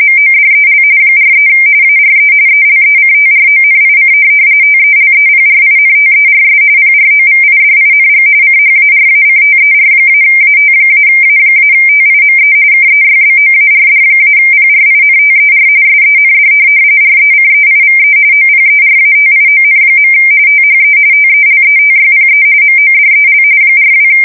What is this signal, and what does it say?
RTTY, about 45.45 baud. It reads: MODEL 43 ARE ELECTRONIC TELEPRINTERS PROVIDING CHARACTER-AT-A-TIME KEYBOARD-PRINTER SEND RECEIVE OPERATION. THE TELETYPE MODEL 42
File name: fsk170 / teletype-corp